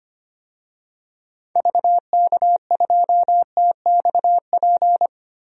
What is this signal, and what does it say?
Morse code, heard as a keyed tone, about 25 wpm, 700 Hz. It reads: VK2TXP
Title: DXCC-100 Call Signs - 6 Characters - 25wpm